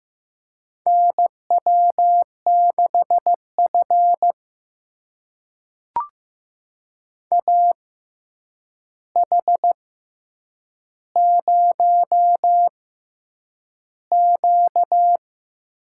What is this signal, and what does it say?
Morse code, heard as a keyed tone, about 15 wpm, spaced as for 8 wpm, 700 Hz. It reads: NW6F AH0Q
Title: USA Call Signs - 4 Characters - 6x Character Spacing - 15wpm